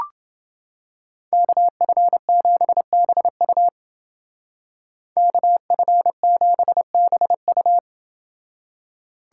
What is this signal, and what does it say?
Morse code, heard as a keyed tone, about 28 wpm, 700 Hz. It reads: KF7BU KF7BU